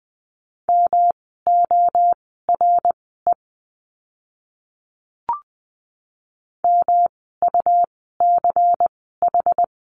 Morse code, heard as a keyed tone, about 20 wpm, spaced as for 15 wpm, 700 Hz. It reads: MORE MUCH
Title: 146 Lesson 24f - Letter y - Words 20wpm